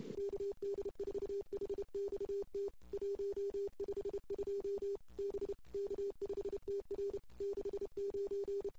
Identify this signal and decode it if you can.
Morse code, heard as a keyed tone, about 28 wpm, 410 Hz. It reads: KD4HXT 152 B K5TR 69
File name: CW / 2008-ss-cw-k5tr-op-n5rz -22:00